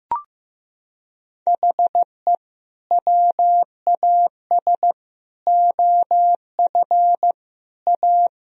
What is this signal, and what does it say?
Morse code, heard as a keyed tone, about 15 wpm, 700 Hz. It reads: HE WAS OF A
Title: Sentences from Top 1000 Words - Review Part 1 - 15wpm